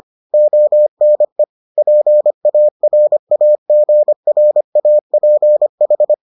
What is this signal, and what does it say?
Morse code, heard as a keyed tone, about 25 wpm, 600 Hz. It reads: ONE PARAGRAPH